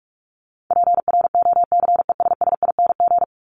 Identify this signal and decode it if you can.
Morse code, heard as a keyed tone, about 44 wpm, 700 Hz. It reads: PROCESSING
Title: ING Suffix - Top 100 Words with Base Words 45wpm